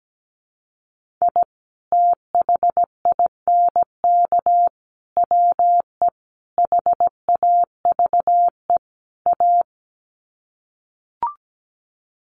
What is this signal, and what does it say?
Morse code, heard as a keyed tone, about 17 wpm, 700 Hz. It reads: I THINK WE HAVE A
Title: Sets of 5 Words Encore 17wpm